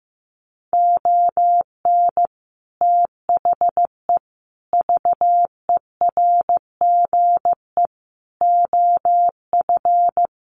Morse code, heard as a keyed tone, about 15 wpm, 700 Hz. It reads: ON THE VERGE OF
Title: Sets of 4 Words 15wpm